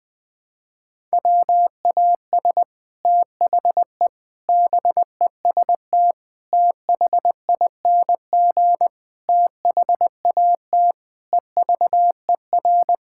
Morse code, heard as a keyed tone, about 20 wpm, 700 Hz. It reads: WAS THE BEST THING THAT EVER
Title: Sets of 8 Words - 20wpm